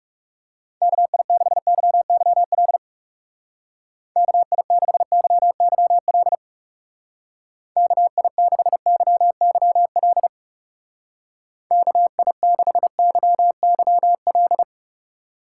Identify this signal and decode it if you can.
Morse code, heard as a keyed tone, about 38 wpm, 700 Hz. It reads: KI6YYL KI6YYL KI6YYL KI6YYL